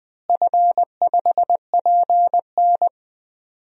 Morse code, heard as a keyed tone, about 20 wpm, 700 Hz. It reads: F5PN
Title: DXCC-100 Call Signs - 4 Characters - Repeated 2x - 20wpm